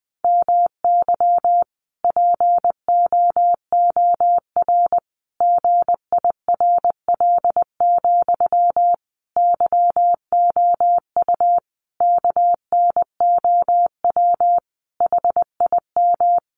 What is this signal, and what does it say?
Morse code, heard as a keyed tone, about 20 wpm, 700 Hz. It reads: MY POOR GIRL, YOU KNOW HIM
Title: Sentences from Top 1000 Words 20wpm